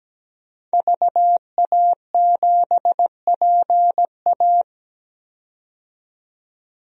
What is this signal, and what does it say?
Morse code, heard as a keyed tone, about 17 wpm, 700 Hz. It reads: VA7PA
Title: DXCC-100 Call Signs - 5 Characters - 17wpm